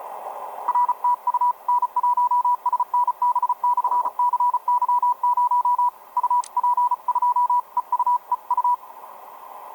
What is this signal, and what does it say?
Morse code, heard as a keyed tone, about 34 wpm, 980 Hz, with radio noise. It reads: RTU N1SNB/CY0 UP2EUEU
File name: n1snb-cy0 ea4dc etc 30m cw 06mar2011 2222